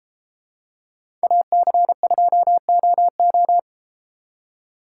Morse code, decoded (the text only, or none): AC2OO